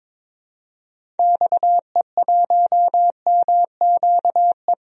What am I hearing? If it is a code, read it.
Morse code, heard as a keyed tone, about 22 wpm, 700 Hz. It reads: XE1MQE
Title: DXCC-100 Call Signs - 6 Characters - Repeated 3x - 22wpm